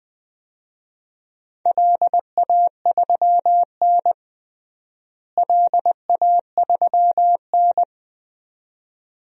Morse code, heard as a keyed tone, about 21 wpm, 700 Hz. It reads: LA3N LA3N
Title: DXCC-100 Call Signs - 4 Characters - Repeated 2x - 20wpm